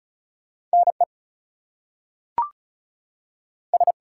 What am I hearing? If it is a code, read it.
Morse code, heard as a keyed tone, about 36 wpm, spaced as for 6 wpm, 700 Hz. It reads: NES
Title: US States and Canadian Provinces 35wpm